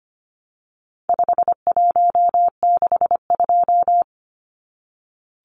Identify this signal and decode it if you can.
Morse code, heard as a keyed tone, about 25 wpm, 700 Hz. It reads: 5162
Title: Numbers - 4-Digit 25wpm